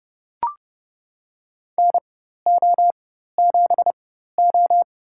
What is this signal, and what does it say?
Morse code, heard as a keyed tone, about 31 wpm, spaced as for 14 wpm, 700 Hz. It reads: NO7O
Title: USA Call Signs - 4x Character Spacing - 30wpm